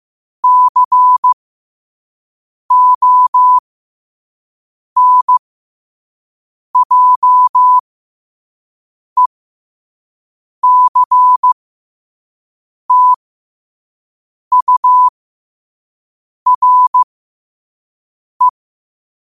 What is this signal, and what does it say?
Morse code, heard as a keyed tone, about 15 wpm, spaced as for 5 wpm, 1000 Hz. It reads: CONJECTURE
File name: qotdfiles / 5wpm